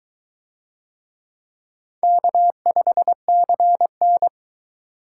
Morse code, heard as a keyed tone, about 22 wpm, 700 Hz. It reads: K5CN